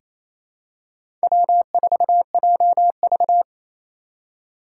Morse code, heard as a keyed tone, about 28 wpm, 700 Hz. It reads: W4JV
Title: USA Call Signs - 4 Characters - 4x Character Spacing - 28wpm